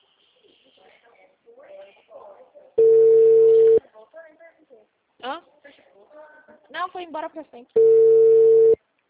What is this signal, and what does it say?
Ringback tone (China, 450 Hz), with speech.